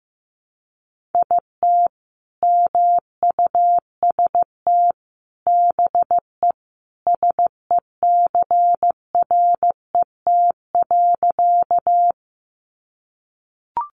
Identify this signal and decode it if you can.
Morse code, heard as a keyed tone, about 15 wpm, 700 Hz. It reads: IT MUST BE SECRET.
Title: Sentences from Top 1200 Words 15wpm